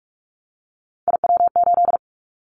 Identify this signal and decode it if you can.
Morse code, heard as a keyed tone, about 46 wpm, 700 Hz. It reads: IW8
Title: Callsign Prefixes for Top 25 DXCC - 45wpm